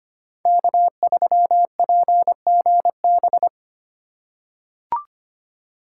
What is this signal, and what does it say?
Morse code, heard as a keyed tone, about 25 wpm, 700 Hz. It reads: K3PGB
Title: USA Call Signs - 5 Characters - 3x Character Spacing - 25wpm